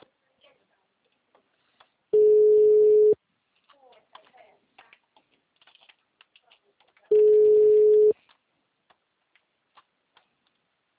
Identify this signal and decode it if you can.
Ringback tone (Europe, 425 Hz), with speech.
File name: out- -2002- - - .22312